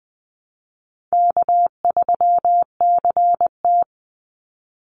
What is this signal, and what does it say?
Morse code, heard as a keyed tone, about 20 wpm, 700 Hz. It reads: K3CT
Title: Easy Call Signs - 20wpm